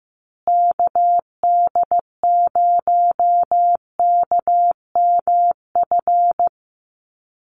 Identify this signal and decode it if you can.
Morse code, heard as a keyed tone, about 15 wpm, 700 Hz. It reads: KD0KMF
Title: USA Call Signs Repeated 2x 15wpm